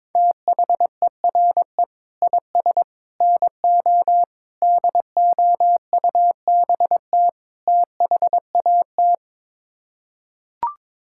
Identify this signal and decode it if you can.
Morse code, heard as a keyed tone, about 21 wpm, 700 Hz. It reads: THERE IS NO DOUBT THAT
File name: practice / 22wpm / Build A Phrase - 5-Word Phrases - 22wpm